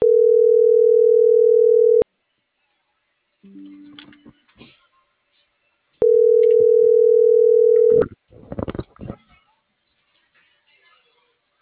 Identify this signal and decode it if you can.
Ringback tone (North America), 440 + 480 Hz.